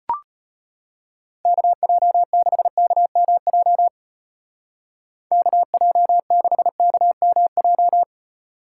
Morse code, heard as a keyed tone, about 30 wpm, 700 Hz. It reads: KJ6KMJ KJ6KMJ
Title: Speed Racing - USA 2x3 Call Signs 25wpm